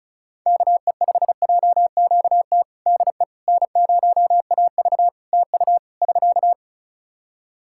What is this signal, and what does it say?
Morse code, heard as a keyed tone, about 36 wpm, 700 Hz. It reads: KE5JQT DE N0AV TU <SK>
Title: Practice QSO - Exchange 4 - Ending - 35wpm